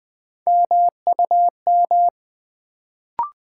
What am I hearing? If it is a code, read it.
Morse code, heard as a keyed tone, about 20 wpm, 700 Hz. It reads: MUM
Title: Build A Word - 1 Letter At A Time - 3-Letter Words - 20wpm